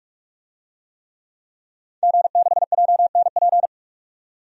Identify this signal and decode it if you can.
Morse code, heard as a keyed tone, about 47 wpm, 700 Hz. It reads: G6JNP
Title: DXCC-100 Call Signs - 5 Characters - 45wpm